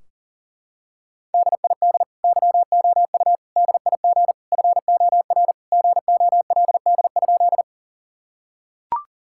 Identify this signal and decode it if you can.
Morse code, heard as a keyed tone, about 37 wpm, 700 Hz. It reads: DID YOU DIG FOR GOLD?
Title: Sentences from Top 1600 Words 40wpm